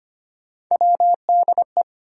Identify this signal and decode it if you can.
Morse code, heard as a keyed tone, about 25 wpm, 700 Hz. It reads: WDE
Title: USA Call Sign Postfixes - 3 Letters 25wpm